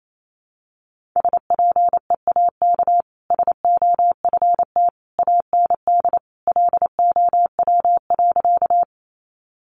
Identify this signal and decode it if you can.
Morse code, heard as a keyed tone, about 28 wpm, 700 Hz. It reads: SPEAK SOFT AND LOW.
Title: Sentences from Top 1500 Words 28wpm